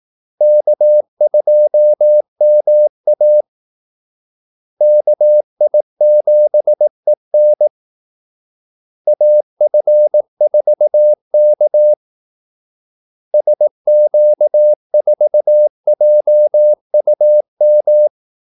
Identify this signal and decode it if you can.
Morse code, heard as a keyed tone, about 18 wpm, 600 Hz. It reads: K2MA KI7EN AF4K SQ4JUM
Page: Kaldesignaler 18 wpm